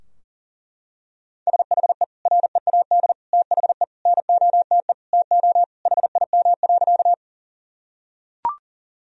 Morse code, heard as a keyed tone, about 37 wpm, 700 Hz. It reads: SHE READ THE NOTE TO HIM.
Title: Sentences from Top 1000 Words - Review Part 2 - 40wpm